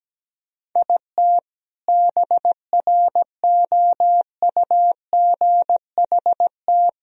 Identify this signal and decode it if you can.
Morse code, heard as a keyed tone, about 17 wpm, 700 Hz. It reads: IT BROUGHT
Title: Sentences from Top 1000 Words - Review Part 3 - 17wpm